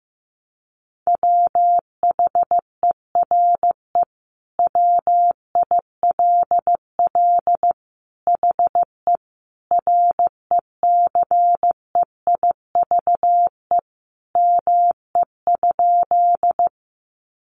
Morse code, heard as a keyed tone, about 15 wpm, 700 Hz. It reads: WHERE WILL HE RECEIVE ME?
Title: Sentences from Top 1000 Words 15wpm